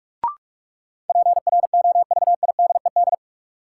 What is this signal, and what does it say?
Morse code, heard as a keyed tone, about 46 wpm, 700 Hz. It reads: PROVIDED
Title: Sentences from Top 1500 Words - New Word Drill - 45wpm